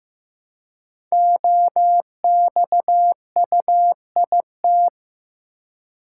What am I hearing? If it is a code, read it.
Morse code, heard as a keyed tone, about 15 wpm, 700 Hz. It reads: OXUIT
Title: Code Groups - 5 Letters - 15wpm